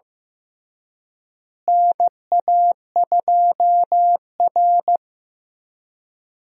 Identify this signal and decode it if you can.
Morse code, heard as a keyed tone, about 15 wpm, 700 Hz. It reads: NA2R
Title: USA Call Signs - 4 Characters - Repeated 3x - 15wpm